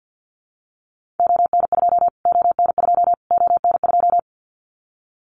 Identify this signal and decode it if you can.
Morse code, heard as a keyed tone, about 50 wpm, 700 Hz. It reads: ON2 ON2 ON2
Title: Callsign Prefixes for Top 25 DXCC - Repeated 3x - 50wpm